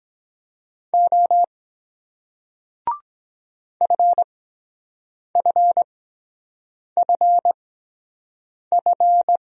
Morse code, heard as a keyed tone, about 23 wpm, spaced as for 7 wpm, 700 Hz. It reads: O FFFF